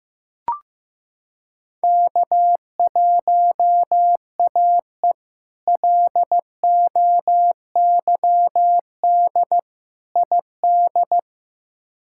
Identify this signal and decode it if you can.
Morse code, heard as a keyed tone, about 15 wpm, 700 Hz. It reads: K1AE LOYD ID
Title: K1USN SST QSO Practice - 15wpm